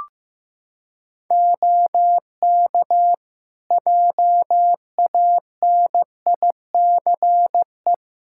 Morse code, heard as a keyed tone, about 15 wpm, 700 Hz. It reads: OK JANICE
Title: K1USN SST QSO Practice 15wpm